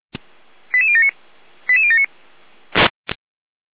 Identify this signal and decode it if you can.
Five-tone selective call (ZVEI), heard as quick runs of five tones, tones of about 70 ms. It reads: 88079 88079